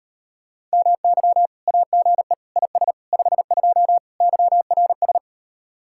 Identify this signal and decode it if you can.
Morse code, heard as a keyed tone, about 38 wpm, 700 Hz. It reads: MY AGE IS 52 YRS